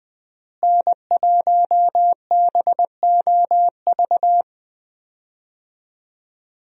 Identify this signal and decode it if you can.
Morse code, heard as a keyed tone, about 20 wpm, 700 Hz. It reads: N1BOV
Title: USA Call Signs - 5 Characters - Repeated 2x - 20wpm